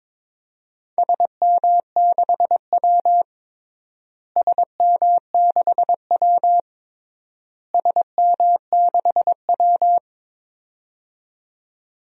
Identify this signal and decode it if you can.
Morse code, heard as a keyed tone, about 22 wpm, 700 Hz. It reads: SM6W SM6W SM6W
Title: DXCC-100 Call Signs - 4 Characters - Repeated 3x - 22wpm